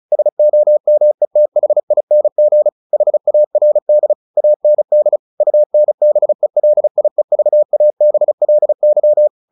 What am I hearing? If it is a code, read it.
Morse code, heard as a keyed tone, about 34 wpm, 600 Hz. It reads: SOMETHING HARD AND UNBELIEVABLY
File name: the old man and the sea / the old man and the sea 1 35wpm